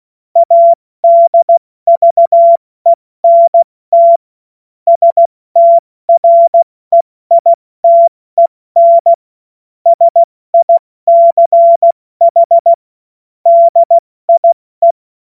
Morse code, heard as a keyed tone, about 16 wpm, 680 Hz. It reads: ADVENT STREITEN SICH DIE
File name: Morsekurs / Witze / Witz 21. März 2023 70 F80